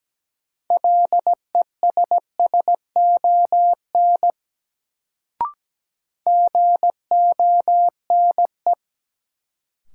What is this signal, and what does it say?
Morse code, heard as a keyed tone, about 17 wpm, 700 Hz. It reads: LESSON GONE
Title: Sentences from Top 1200 Words - New Word Drill - 17wpm